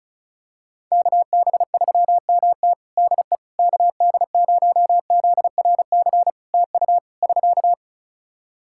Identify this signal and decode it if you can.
Morse code, heard as a keyed tone, about 34 wpm, 700 Hz. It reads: KB3MT DE KD0ZRC TU <SK>
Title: Practice QSO - Exchange 4 - Ending - 35wpm